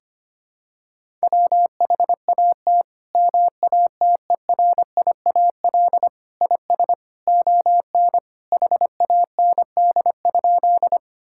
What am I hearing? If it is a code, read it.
Morse code, heard as a keyed tone, about 25 wpm, 700 Hz. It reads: WHAT MATERIAL IS ON HAND?
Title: Sentences from Top 1000 Words - Review Part 5 - 25wpm